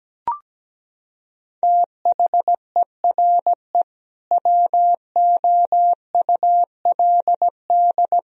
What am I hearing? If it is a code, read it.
Morse code, heard as a keyed tone, about 17 wpm, 700 Hz. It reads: THERE WOULD
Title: Sentences from Top 1000 Words - Review Part 4 - 17wpm